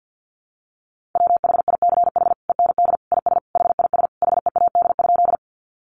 Morse code, heard as a keyed tone, about 50 wpm, 700 Hz. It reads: WHICH END IS HIS HEAD?
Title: Sentences from Top 300 Words 50wpm